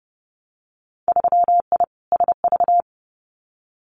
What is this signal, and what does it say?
Morse code, heard as a keyed tone, about 30 wpm, 700 Hz. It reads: 3I SV
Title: Winter Field Day Exchanges Encore 30wpm